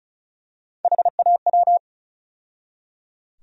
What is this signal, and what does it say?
Morse code, heard as a keyed tone, about 34 wpm, 700 Hz. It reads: HAW
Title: USA Call Sign Postfixes - 3 Letters 35wpm